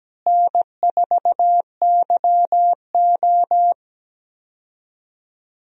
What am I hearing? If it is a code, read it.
Morse code, heard as a keyed tone, about 17 wpm, 700 Hz. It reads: N4YO
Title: USA Call Signs - 4 Characters - Repeated 3x - 17wpm